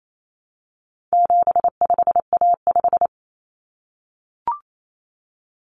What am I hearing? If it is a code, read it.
Morse code, heard as a keyed tone, about 28 wpm, 700 Hz. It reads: 75A5
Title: Common Cut Numbers - 4-Digit 28wpm